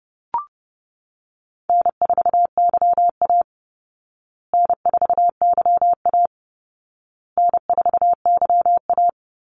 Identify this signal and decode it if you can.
Morse code, heard as a keyed tone, about 30 wpm, 700 Hz. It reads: N4YA N4YA N4YA
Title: Call Signs Repeated 2x - 30wpm